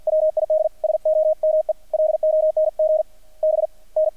Morse code, heard as a keyed tone, about 50 wpm, spaced as for 43 wpm, 620 Hz, with radio noise. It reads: WID SOME LONG BN
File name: archived / CRSnet